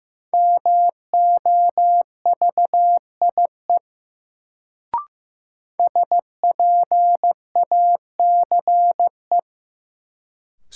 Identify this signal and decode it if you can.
Morse code, heard as a keyed tone, about 15 wpm, 700 Hz. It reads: MOVIE SPACE
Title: Sentences from Top 1000 Words - Review Part 5 - New Word Drill - 15wpm